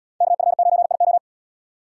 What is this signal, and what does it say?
Morse code, heard as a keyed tone, about 77 wpm, 700 Hz. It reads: KF9IO